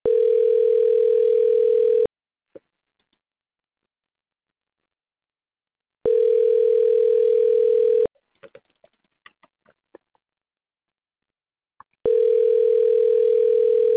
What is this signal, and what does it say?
Ringback tone (North America), 440 + 480 Hz.